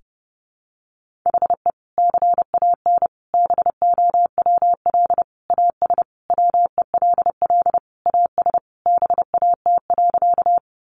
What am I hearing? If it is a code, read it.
Morse code, heard as a keyed tone, about 30 wpm, 700 Hz. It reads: HE CAN BOWL AS WELL AS BAT.
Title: Sentences from Top 1500 Words 30wpm